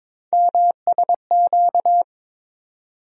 Morse code, heard as a keyed tone, about 22 wpm, 700 Hz. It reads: MSQ
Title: USA Call Sign Postfixes - 3 Letters 22wpm